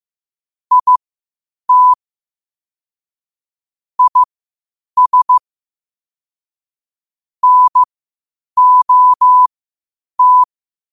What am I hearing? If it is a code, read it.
Morse code, heard as a keyed tone, about 15 wpm, spaced as for 8 wpm, 1000 Hz. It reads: IT IS NOT